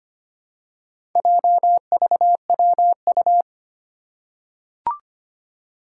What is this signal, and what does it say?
Morse code, heard as a keyed tone, about 25 wpm, 700 Hz. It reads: JVWU
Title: Four Challenging Letter-Number Combinations 25wpm